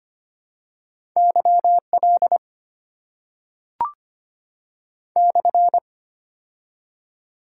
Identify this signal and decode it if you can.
Morse code, heard as a keyed tone, about 25 wpm, 700 Hz. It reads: Y L /
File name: Top 90 QSO Elements - 6x Character Spacing - 25wpm